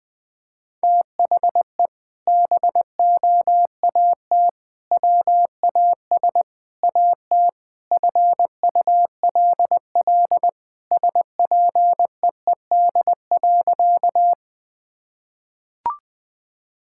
Morse code, heard as a keyed tone, about 21 wpm, 700 Hz. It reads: THE BOAT WAS AT FULL SPEED.
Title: Sentences from Top 1000 Words - Review Part 5 - 20wpm